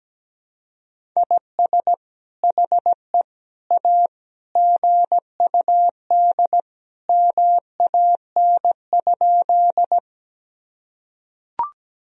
Morse code, heard as a keyed tone, about 17 wpm, 700 Hz. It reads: IS HE A GUD MAN?
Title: Sentences from Top 200 Words and Abbreviations 17wpm